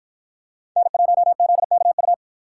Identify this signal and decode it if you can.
Morse code, heard as a keyed tone, about 50 wpm, 700 Hz. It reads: N1ZKU